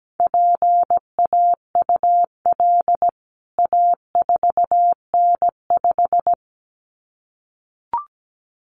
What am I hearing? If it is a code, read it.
Morse code, heard as a keyed tone, about 17 wpm, 700 Hz. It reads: PAUL A4N5
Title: CWT Exchange 17wpm